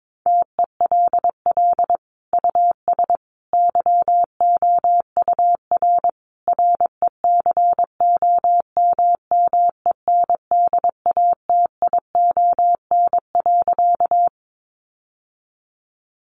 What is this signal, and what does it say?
Morse code, heard as a keyed tone, about 22 wpm, 700 Hz. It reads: TELL US YOUR RECOMMENDATION.
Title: Sentences from Top 1400 Words 22wpm